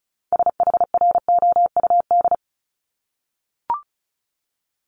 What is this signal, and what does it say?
Morse code, heard as a keyed tone, about 36 wpm, 700 Hz. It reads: SHROUD
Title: Multi-letter Phonemes Repeated 3x 35wpm